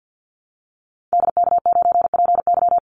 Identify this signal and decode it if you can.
Morse code, heard as a keyed tone, about 50 wpm, 700 Hz. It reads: DK9PY
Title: Call Signs Repeated 2x - 50wpm